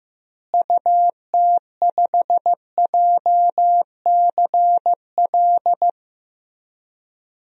Morse code, heard as a keyed tone, about 15 wpm, 700 Hz. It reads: UT5JCL
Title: DXCC-100 Call Signs - 6 Characters - Repeated 2x - 15wpm